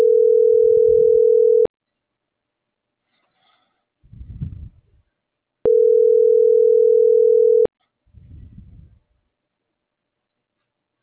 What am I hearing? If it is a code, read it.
Ringback tone (North America), 440 + 480 Hz.